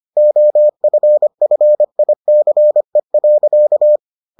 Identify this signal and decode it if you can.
Morse code, heard as a keyed tone, about 25 wpm, 600 Hz. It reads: OFFICE.